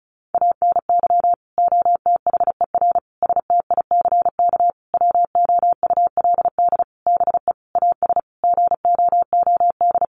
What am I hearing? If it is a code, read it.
Morse code, heard as a keyed tone, about 34 wpm, 700 Hz. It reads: ANY OTHER STICK WOULD BE AS GOOD
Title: Sentences from Top 1200 Words 35wpm